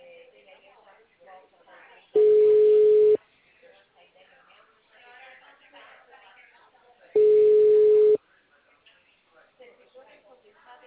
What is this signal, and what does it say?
Ringback tone (Europe, 425 Hz), with speech.
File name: out- -2009- - - .0